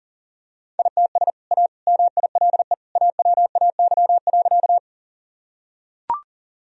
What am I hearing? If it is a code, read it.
Morse code, heard as a keyed tone, about 40 wpm, 700 Hz. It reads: ITS A MILE AWAY.
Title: Sentences from Top 1000 Words - Review Part 1 - 40wpm